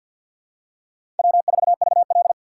Morse code, heard as a keyed tone, about 52 wpm, 700 Hz. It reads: W4VL
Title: USA Call Signs - 4 Characters - 2x Character Spacing - 50wpm